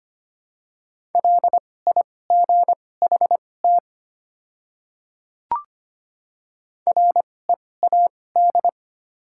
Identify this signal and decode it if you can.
Morse code, heard as a keyed tone, about 25 wpm, spaced as for 18 wpm, 700 Hz. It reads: LIGHT READ
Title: 2x Character Spacing - Top 300 Words - 25wpm